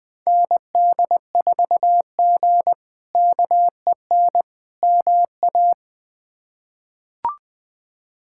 Morse code, heard as a keyed tone, about 20 wpm, 700 Hz. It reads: ND4G KEN MA
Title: K1USN SST QSO Practice - 20wpm